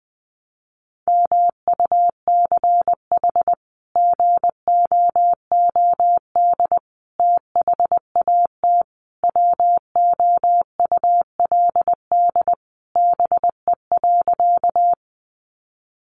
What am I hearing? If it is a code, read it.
Morse code, heard as a keyed tone, about 20 wpm, 700 Hz. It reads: MUCH GOOD THAT WOULD BE.
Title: Sentences from Top 200 Words 20wpm